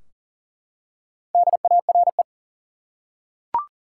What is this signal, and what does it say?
Morse code, heard as a keyed tone, about 40 wpm, 700 Hz. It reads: DARE
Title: Four Letter Words - 40wpm